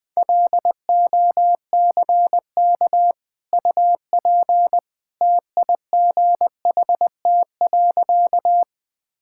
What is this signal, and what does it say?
Morse code, heard as a keyed tone, about 20 wpm, 700 Hz. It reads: LOCK UP TIGHT.
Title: Easy Sentences with Hints - 20wpm